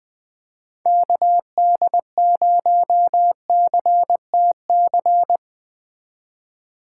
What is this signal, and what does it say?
Morse code, heard as a keyed tone, about 20 wpm, 700 Hz. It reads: KD0CTC